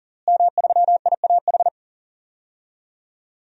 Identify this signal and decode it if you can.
Morse code, heard as a keyed tone, about 40 wpm, 700 Hz. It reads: M3IAH
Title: DXCC-100 Call Signs - 5 Characters - Repeated 2x - 40wpm